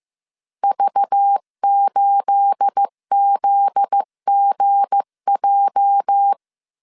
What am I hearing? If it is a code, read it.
Morse code, heard as a keyed tone, about 15 wpm, 800 Hz. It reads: V8ZGJ
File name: CW / DET / 1 9 1500C S B 14wpm 3302